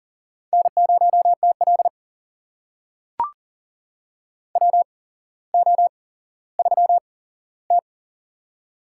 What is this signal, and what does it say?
Morse code, heard as a keyed tone, about 42 wpm, spaced as for 17 wpm, 700 Hz. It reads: N0TL WO3T
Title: USA Call Signs - 4 Characters - 8x Character Spacing - 40wpm